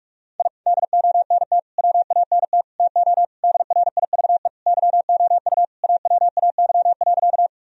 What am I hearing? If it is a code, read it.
Morse code, heard as a keyed tone, about 44 wpm, 700 Hz. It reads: I DONT WANT TO DRIVE YOU AWAY.
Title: Sentences from Top 1000 Words - Review Part 2 - 45wpm